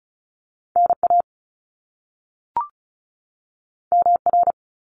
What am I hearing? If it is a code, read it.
Morse code, heard as a keyed tone, about 36 wpm, 700 Hz. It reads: NA MR
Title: Top Two Letter Combinations in Words 35wpm